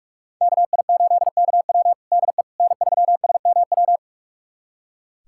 Morse code, heard as a keyed tone, about 46 wpm, 700 Hz. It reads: KI8KW DE N3SMW
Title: USA Callsign DE Callsign - 5 Characters - 45wpm